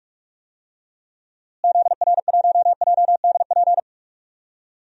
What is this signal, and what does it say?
Morse code, heard as a keyed tone, about 41 wpm, 700 Hz. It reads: ZR1JDP
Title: DXCC-100 Call Signs - 6 Characters - Repeated 2x - 45wpm